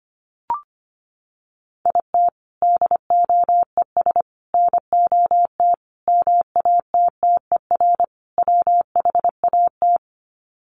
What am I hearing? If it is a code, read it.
Morse code, heard as a keyed tone, about 25 wpm, 690 Hz. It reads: IT DOES NOT MATTER WHAT
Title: Sets of 5 Words Encore 25wpm